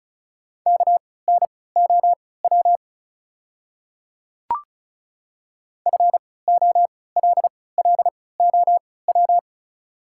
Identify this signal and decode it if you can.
Morse code, heard as a keyed tone, about 34 wpm, spaced as for 20 wpm, 700 Hz. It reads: KNOW FOLLOW
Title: 3x Character Spacing - Top 300 Words - 35wpm